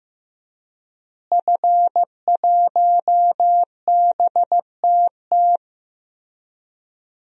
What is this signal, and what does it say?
Morse code, heard as a keyed tone, about 16 wpm, 700 Hz. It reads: F1BTT
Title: DXCC-100 Call Signs - 5 Characters - Repeated 3x - 15wpm